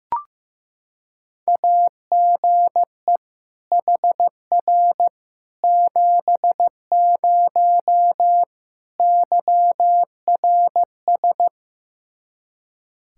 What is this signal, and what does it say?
Morse code, heard as a keyed tone, about 16 wpm, 700 Hz. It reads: AGE HR 70 YRS
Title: Practice QSO - Exchange 3 - More Info - 15wpm